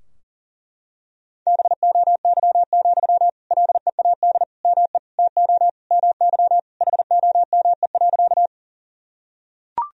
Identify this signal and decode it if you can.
Morse code, heard as a keyed tone, about 40 wpm, 700 Hz. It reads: BOY, LEAD ME TO MY HOME.
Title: Sentences from Top 1000 Words - Review Part 2 - 40wpm